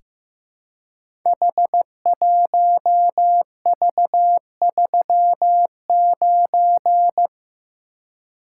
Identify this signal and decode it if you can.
Morse code, heard as a keyed tone, about 15 wpm, 700 Hz. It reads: H1V39
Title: Code Groups - 5 Letter:Numbers - 15wpm